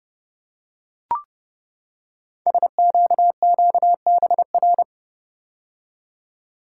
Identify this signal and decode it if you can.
Morse code, heard as a keyed tone, about 29 wpm, 690 Hz. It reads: SQQBR
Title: Code Groups - 5 Letters - 30wpm